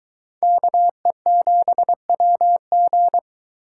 Morse code, heard as a keyed tone, about 23 wpm, 700 Hz. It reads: KE7WG